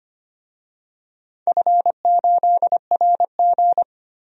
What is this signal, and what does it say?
Morse code, heard as a keyed tone, about 25 wpm, 700 Hz. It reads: F8RG
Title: DXCC-100 Call Signs - 4 Characters - 25wpm